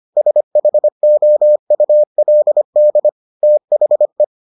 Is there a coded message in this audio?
Yes: Morse code from 0.2 to 4.2 s, heard as a keyed tone, about 25 wpm, 600 Hz. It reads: SHOULD THE